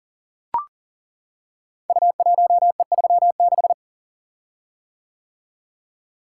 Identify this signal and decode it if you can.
Morse code, heard as a keyed tone, about 39 wpm, 700 Hz. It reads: U1E36
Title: Code Groups - 5 Letter:Numbers - 40wpm